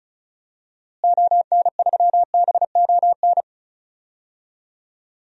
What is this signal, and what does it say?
Morse code, heard as a keyed tone, about 36 wpm, 700 Hz. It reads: ON3BON
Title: DXCC-100 Call Signs - 6 Characters - Repeated 2x - 35wpm